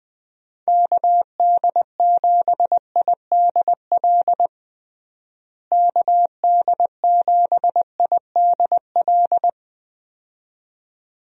Morse code, heard as a keyed tone, about 20 wpm, 700 Hz. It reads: KD7IDL KD7IDL
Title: USA Call Signs - 6 Characters - Repeated 2x - 20wpm